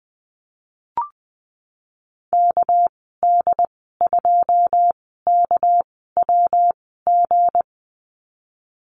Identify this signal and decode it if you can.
Morse code, heard as a keyed tone, about 20 wpm, spaced as for 15 wpm, 700 Hz. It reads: KD2KWG
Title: 231 Lesson 37g - Letter x - Call Signs 20wpm